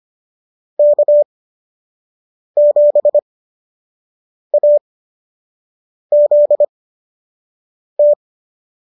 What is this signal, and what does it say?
Morse code, heard as a keyed tone, about 25 wpm, spaced as for 6 wpm, 590 Hz. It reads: K7AZT